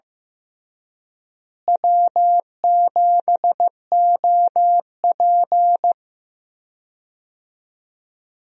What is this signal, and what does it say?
Morse code, heard as a keyed tone, about 15 wpm, 700 Hz. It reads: W7OP
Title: USA Call Signs - 4 Characters - Repeated 3x - 15wpm